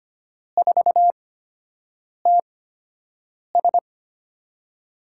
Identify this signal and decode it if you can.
Morse code, heard as a keyed tone, about 25 wpm, spaced as for 7 wpm, 700 Hz. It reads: 4TS
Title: USA Call Signs - 5 Characters - 8x Character Spacing - 25wpm